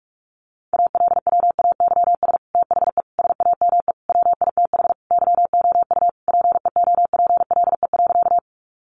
Morse code, heard as a keyed tone, about 44 wpm, 700 Hz. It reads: ALWAYS THE SAME WITH YOU PEOPLE.
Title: Sentences from Top 300 Words 45wpm